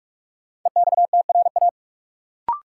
Morse code, heard as a keyed tone, about 43 wpm, 700 Hz. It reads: EXTRA
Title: Sentences from Top 1000 Words - New Word Drill - 45wpm